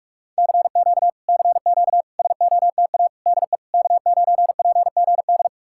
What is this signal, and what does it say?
Morse code, heard as a keyed tone, about 59 wpm, 700 Hz. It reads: CQ CQ SOTA DE K9PGD